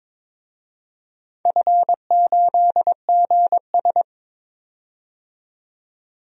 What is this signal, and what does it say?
Morse code, heard as a keyed tone, about 22 wpm, 700 Hz. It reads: F8GS
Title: DXCC-100 Call Signs - 4 Characters - 22wpm